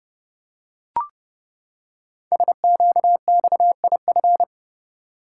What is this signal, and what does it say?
Morse code, heard as a keyed tone, about 30 wpm, 700 Hz. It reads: SQXIF
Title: Code Groups - 5 Letter:Numbers - 30wpm